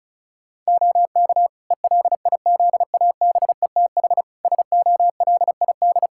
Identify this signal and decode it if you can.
Morse code, heard as a keyed tone, about 36 wpm, 700 Hz. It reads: OK ELIZABETH SOLID
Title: Practice QSO - Exchange 3 - More Info - 35wpm